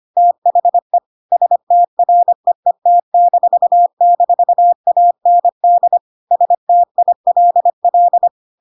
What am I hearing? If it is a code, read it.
Morse code, heard as a keyed tone, about 25 wpm, 700 Hz. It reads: THE STREET--AND STILL
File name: Pinocchio / Pinocchio ch1 25 25 ik1hge